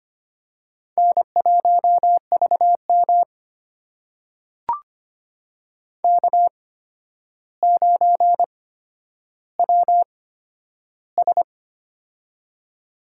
Morse code, heard as a keyed tone, about 25 wpm, spaced as for 11 wpm, 700 Hz. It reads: N1VM K9WS
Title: USA Call Signs - 4 Characters - 8x Character Spacing - 25wpm